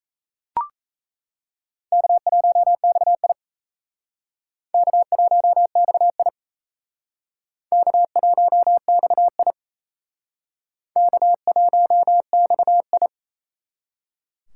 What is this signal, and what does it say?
Morse code, heard as a keyed tone, about 37 wpm, 700 Hz. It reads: K1XI K1XI K1XI K1XI